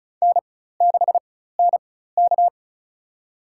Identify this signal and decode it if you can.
Morse code, heard as a keyed tone, about 36 wpm, spaced as for 16 wpm, 700 Hz. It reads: N6NK
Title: USA Call Signs - 4 Characters - 4x Character Spacing - 35wpm